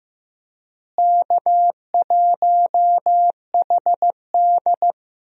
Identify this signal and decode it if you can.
Morse code, heard as a keyed tone, about 15 wpm, 700 Hz. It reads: K1HD
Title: USA Call Signs - 4 Characters - Repeated 3x - 15wpm